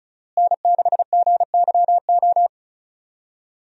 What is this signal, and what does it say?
Morse code, heard as a keyed tone, about 34 wpm, 700 Hz. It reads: N6GYO
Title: USA Call Signs - 5 Characters - Repeated 3x - 35wpm